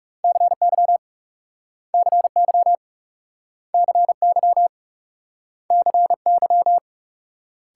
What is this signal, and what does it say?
Morse code, heard as a keyed tone, about 38 wpm, 700 Hz. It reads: CY CY CY CY